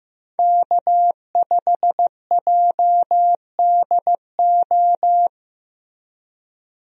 Morse code, heard as a keyed tone, about 16 wpm, 700 Hz. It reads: K5JDO
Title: USA Call Signs - 5 Characters - Repeated 3x - 15wpm